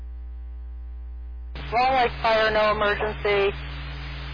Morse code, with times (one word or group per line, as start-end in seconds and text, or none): none